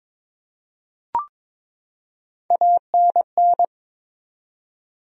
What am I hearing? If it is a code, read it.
Morse code, heard as a keyed tone, about 22 wpm, 700 Hz. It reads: ANN
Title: USA Common Names - Max 4 Letters - 22wpm